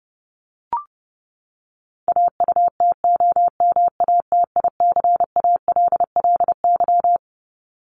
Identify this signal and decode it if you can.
Morse code, heard as a keyed tone, about 30 wpm, 700 Hz. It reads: AUTOMATICALLY
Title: Top 1751-2000 Words 30wpm